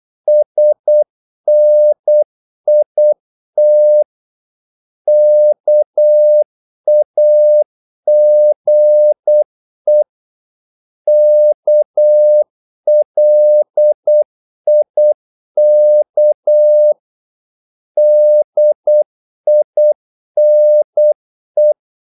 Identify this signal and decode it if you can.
Morse code, heard as a keyed tone, about 8 wpm, 600 Hz. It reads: SNIT KAGE KLIK DINE
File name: Korte ord DK 8wpm